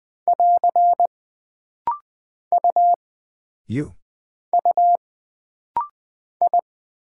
Morse code, heard as a keyed tone, about 20 wpm, 700 Hz. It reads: <AR> U U I
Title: Single Character (Letter:Number:Punctuation:Prosign) - Rapid-Fire - 20wpm